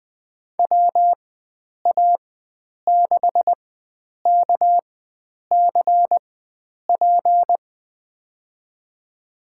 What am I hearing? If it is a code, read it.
Morse code, heard as a keyed tone, about 20 wpm, spaced as for 9 wpm, 700 Hz. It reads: WA6KCP
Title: USA Call Signs - 6 Characters - 4x Character Spacing - 20wpm